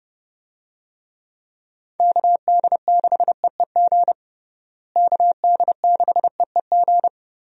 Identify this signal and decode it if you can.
Morse code, heard as a keyed tone, about 32 wpm, 700 Hz. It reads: KD6EEG KD6EEG
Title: USA Call Signs - 6 Characters - Repeated 2x - 30wpm